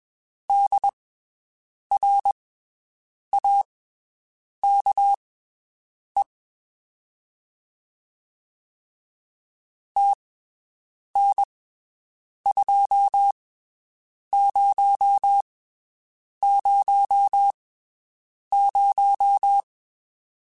Morse code, heard as a keyed tone, about 21 wpm, spaced as for 7 wpm, 800 Hz. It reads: DRAKE TN2000
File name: (spaced) / QSOs Spaced-mp3s / WD0FBT 1 @20 sp